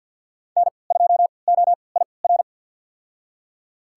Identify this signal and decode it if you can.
Morse code, heard as a keyed tone, about 52 wpm, spaced as for 29 wpm, 700 Hz. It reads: N2OIR
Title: USA Call Signs - 5 Characters - 3x Character Spacing - 50wpm